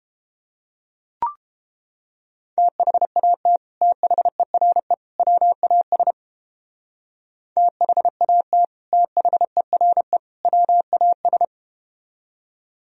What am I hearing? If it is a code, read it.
Morse code, heard as a keyed tone, about 31 wpm, 700 Hz. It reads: THAT THERE WAS THAT THERE WAS